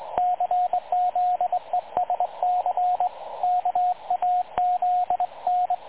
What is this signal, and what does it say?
Morse code, heard as a keyed tone, about 21 wpm, 720 Hz, with radio noise. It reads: CZESC KAZN
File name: dlazaawansowanych / nagraniazpasma / websdr recording 3565.0kHz